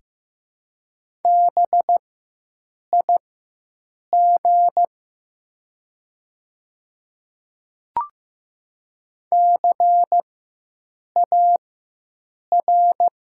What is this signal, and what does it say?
Morse code, heard as a keyed tone, about 15 wpm, spaced as for 7 wpm, 700 Hz. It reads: BIG CAR
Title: 4x Character Spacing - Top 300 Words - 15wpm